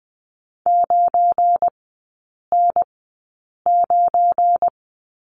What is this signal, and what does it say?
Morse code, heard as a keyed tone, about 20 wpm, spaced as for 8 wpm, 700 Hz. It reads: 9N9
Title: 178 Lesson 29e - Slash - Three Characters 20wpm